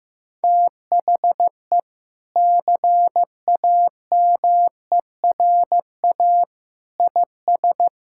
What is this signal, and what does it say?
Morse code, heard as a keyed tone, about 15 wpm, 700 Hz. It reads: THE CAMERA IS
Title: Sentences from Top 1000 Words - Review Part 5 - 15wpm